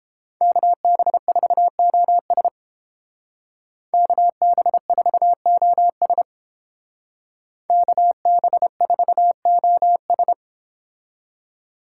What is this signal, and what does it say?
Morse code, heard as a keyed tone, about 28 wpm, 700 Hz. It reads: KB4OS KB4OS KB4OS